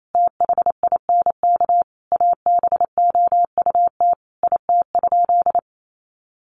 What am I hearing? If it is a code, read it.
Morse code, heard as a keyed tone, about 28 wpm, 700 Hz. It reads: THINK ABOUT IT?
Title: Sentences from Top 200 Words 28wpm